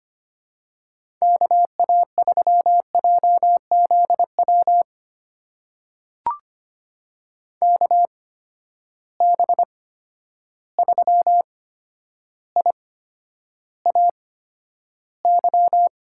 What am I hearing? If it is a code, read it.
Morse code, heard as a keyed tone, about 25 wpm, spaced as for 11 wpm, 700 Hz. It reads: KA3JZW KB3IAY
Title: USA Call Signs - 6 Characters - 8x Character Spacing - 25wpm